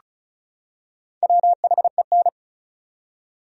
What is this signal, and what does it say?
Morse code, heard as a keyed tone, about 36 wpm, 700 Hz. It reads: WHEN
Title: Top 100 Words 35wpm